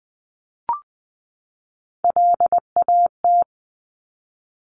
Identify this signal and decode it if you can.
Morse code, heard as a keyed tone, about 20 wpm, 700 Hz. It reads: LAT
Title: Top Three Letter Combinations in Words 20wpm